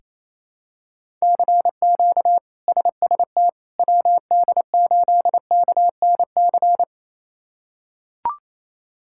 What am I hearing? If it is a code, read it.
Morse code, heard as a keyed tone, about 28 wpm, 700 Hz. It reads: CQ SST WD8KNC
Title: K1USN SST QSO Practice - 28wpm